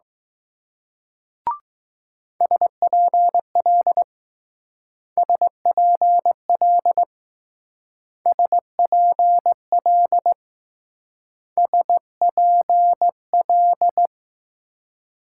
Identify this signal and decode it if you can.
Morse code, heard as a keyed tone, about 19 wpm, 700 Hz. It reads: SPL SPL SPL SPL